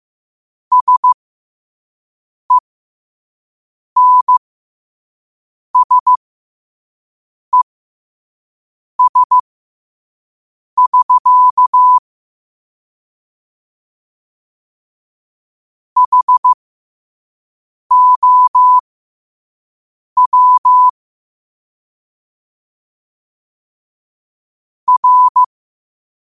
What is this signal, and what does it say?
Morse code, heard as a keyed tone, about 15 wpm, spaced as for 5 wpm, 1000 Hz. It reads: SENSES<SK> HOW R